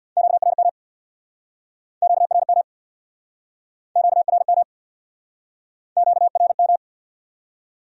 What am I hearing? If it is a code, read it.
Morse code, heard as a keyed tone, about 69 wpm, 700 Hz. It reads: QRM QRM QRM QRM